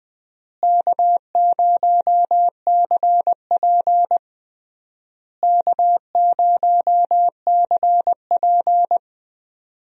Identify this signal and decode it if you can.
Morse code, heard as a keyed tone, about 20 wpm, 690 Hz. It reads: K0CP K0CP
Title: USA Call Signs - 4 Characters - Repeated 3x - 20wpm